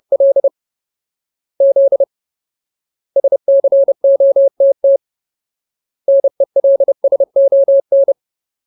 Morse code, heard as a keyed tone, about 30 wpm, 550 Hz. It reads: L Z SCOTT NELSON
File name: Ultimate Word List / 30WPM Ultimate